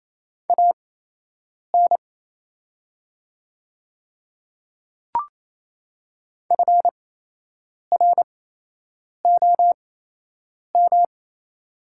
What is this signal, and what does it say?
Morse code, heard as a keyed tone, about 28 wpm, spaced as for 8 wpm, 690 Hz. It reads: AN FROM
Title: 8x Character Spacing - Top 100 Words - 28wpm